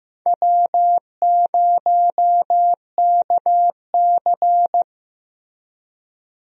Morse code, heard as a keyed tone, about 15 wpm, 700 Hz. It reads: W0KC
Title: USA Call Signs - 4 Characters - Repeated 3x - 15wpm